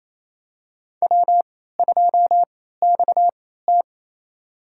Morse code, heard as a keyed tone, about 28 wpm, spaced as for 16 wpm, 700 Hz. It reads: W2XT
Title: USA Call Signs - 4 Characters - 3x Character Spacing - 28wpm